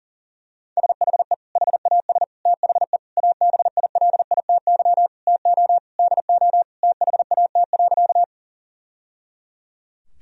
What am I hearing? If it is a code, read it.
Morse code, heard as a keyed tone, about 40 wpm, 700 Hz. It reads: SHE HAS THE ABILITY TO DO THAT.
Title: Sentences from Top 1000 Words - Review Part 3 - 40wpm